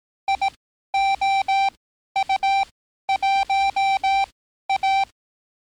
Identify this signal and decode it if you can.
Morse code, heard as a keyed tone, about 18 wpm, spaced as for 12 wpm, 780 Hz. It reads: IOU1A
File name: training / NAVPERS 11049to11060 / NAVPERS 11054 RA RB